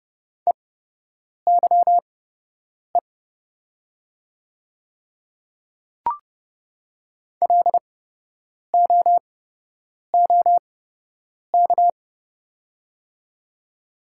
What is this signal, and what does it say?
Morse code, heard as a keyed tone, about 30 wpm, spaced as for 8 wpm, 700 Hz. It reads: EYE LOOK
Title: 8x Character Spacing - Top 300 Words - 30wpm